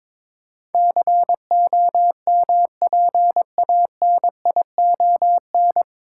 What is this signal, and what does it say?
Morse code, heard as a keyed tone, about 21 wpm, 700 Hz. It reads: COMPANION
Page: Morse Code Ninja practice 22wpm